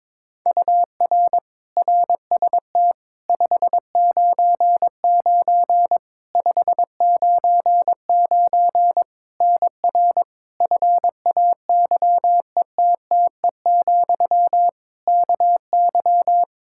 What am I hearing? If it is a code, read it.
Morse code, heard as a keyed tone, about 22 wpm, 700 Hz. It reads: UR RST 599 599 NR FAYETTE, KY
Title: Practice QSO - Exchange 1 - Beginning - 22wpm